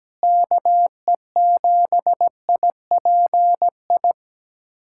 Morse code, heard as a keyed tone, about 17 wpm, 700 Hz. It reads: KE7IPI
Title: USA Callsign DE Callsign - 6 Characters - 17wpm